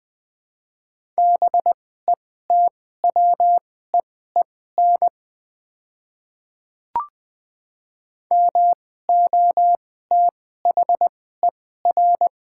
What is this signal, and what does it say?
Morse code, heard as a keyed tone, about 20 wpm, spaced as for 14 wpm, 690 Hz. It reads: BETWEEN MOTHER
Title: 2x Character Spacing - Top 500 Words - 20wpm